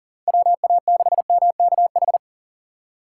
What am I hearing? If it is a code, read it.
Morse code, heard as a keyed tone, about 40 wpm, 700 Hz. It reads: WA6MKH
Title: USA Call Signs - 6 Characters - 3x Character Spacing - 40wpm